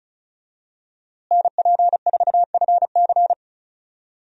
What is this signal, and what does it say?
Morse code, heard as a keyed tone, about 36 wpm, 700 Hz. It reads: NP4FC
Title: USA Call Signs - 5 Characters - Repeated 3x - 35wpm